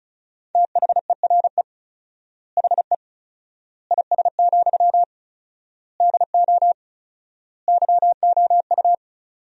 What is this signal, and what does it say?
Morse code, heard as a keyed tone, about 34 wpm, 700 Hz. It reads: THERE HE IS, DO YOU
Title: 4x Word-Spacing - Sentences from Top 100 Words - 35wpm